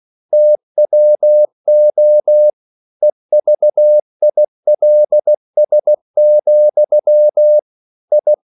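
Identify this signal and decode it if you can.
Morse code, heard as a keyed tone, about 16 wpm, 600 Hz. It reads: TWO EVILS, I
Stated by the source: English quotations sent in CW as practice at 16 words per minute.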